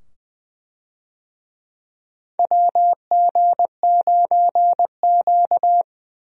Morse code, heard as a keyed tone, about 21 wpm, 700 Hz. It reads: WG9Q
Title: USA Call Signs - 4 Characters - 20wpm